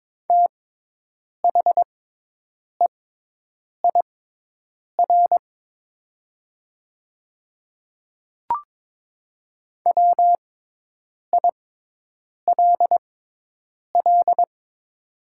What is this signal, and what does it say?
Morse code, heard as a keyed tone, about 22 wpm, spaced as for 8 wpm, 700 Hz. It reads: THEIR WILL
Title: 6x Character Spacing - Top 100 Words - 22wpm